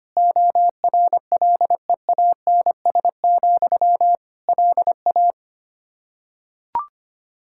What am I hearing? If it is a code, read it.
Morse code, heard as a keyed tone, about 25 wpm, 700 Hz. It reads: ORLEANS, LA
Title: Practice QSO - Exchange 1 - Beginning - 25wpm